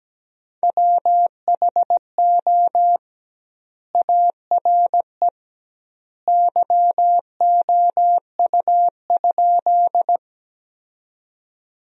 Morse code, heard as a keyed tone, about 17 wpm, 700 Hz. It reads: WHO ARE YOU?
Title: 2x Word-Spacing - Sentences from Top 100 Words - 17wpm